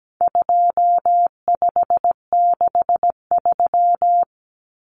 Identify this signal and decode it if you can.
Morse code, heard as a keyed tone, about 17 wpm, 700 Hz. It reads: 2563
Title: Common Cut Numbers - 4-Digit 17wpm